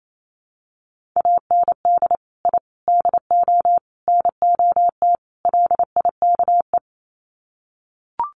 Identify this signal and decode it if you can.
Morse code, heard as a keyed tone, about 28 wpm, 700 Hz. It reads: AND I DO NOT LIKE
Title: Sets of 5 Words Encore2 28wpm